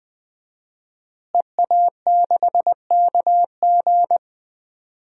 Morse code, heard as a keyed tone, about 21 wpm, 700 Hz. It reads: EA6KG
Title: DXCC-100 Call Signs - 5 Characters - Repeated 2x - 20wpm